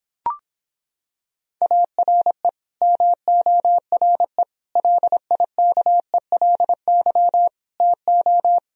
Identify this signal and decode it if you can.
Morse code, heard as a keyed tone, about 26 wpm, 700 Hz. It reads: ARE MORE LIKELY TO